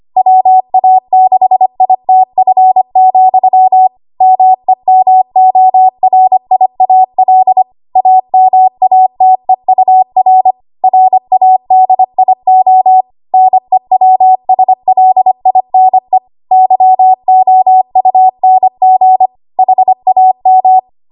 Morse code, heard as a keyed tone, about 25 wpm, 750 Hz. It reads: WA6ITF, MEMORIAL AMATEUR RADIO NEWSLINE YOUNG HAM
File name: Archive / 25 WPM / 25WPM